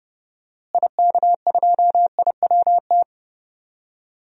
Morse code, heard as a keyed tone, about 30 wpm, 700 Hz. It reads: IK2IWT
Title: DXCC-100 Call Signs - 6 Characters - Repeated 3x - 30wpm